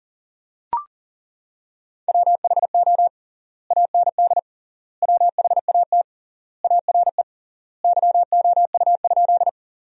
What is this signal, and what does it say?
Morse code, heard as a keyed tone, about 40 wpm, 700 Hz. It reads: WHO AND WHAT ARE YOU?
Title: 3x Word-Spacing - Sentences from Top 100 Words - 40wpm